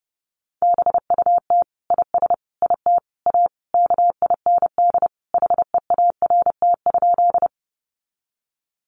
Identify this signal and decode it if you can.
Morse code, heard as a keyed tone, about 29 wpm, 700 Hz. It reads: BUT IS IT A KIND HEART?
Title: Sentences from Top 1200 Words 30wpm